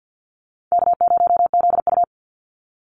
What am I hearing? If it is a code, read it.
Morse code, heard as a keyed tone, about 50 wpm, 700 Hz. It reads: K0ZU
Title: Easy Call Signs Repeated 2x - 50wpm